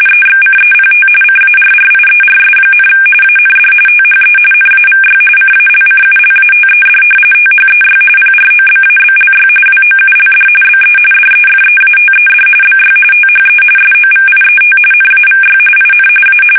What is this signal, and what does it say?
RTTY, about 45.45 baud. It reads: EXCEPT CARRIERS, AND THE NAVY CALCULATED THEY PROBABLY WOULD BE DEFEATED BY THE SOVIET NAVY IN A
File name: fsk850 / usnavy5